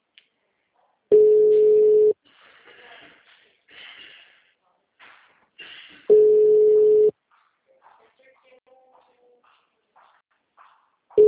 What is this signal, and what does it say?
Ringback tone (Europe, 425 Hz), with speech.